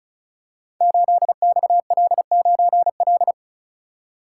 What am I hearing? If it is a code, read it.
Morse code, heard as a keyed tone, about 36 wpm, 700 Hz. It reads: 8XL9L
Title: Code Groups - 5 Letter:Numbers - 35wpm